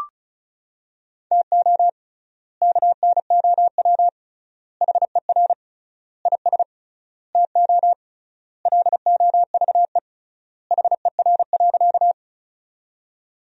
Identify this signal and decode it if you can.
Morse code, heard as a keyed tone, about 36 wpm, 700 Hz. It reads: TO KNOW HER IS TO LOVE HER.
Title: 3x Word-Spacing - Sentences from Top 400 Words - 35wpm